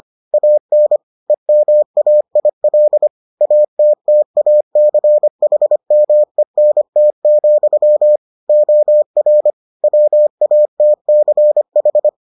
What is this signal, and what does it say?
Morse code, heard as a keyed tone, about 25 wpm, 600 Hz. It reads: AN EMAIL ATTACHMENT, OR WATCH